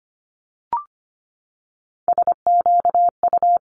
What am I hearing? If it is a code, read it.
Morse code, heard as a keyed tone, about 25 wpm, 700 Hz. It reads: SQU
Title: Multi-letter Phonemes 25wpm